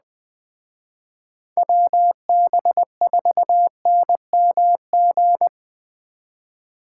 Morse code, heard as a keyed tone, about 20 wpm, 700 Hz. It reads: WB4NMG
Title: Speed Racing - USA 2x3 Call Signs 17wpm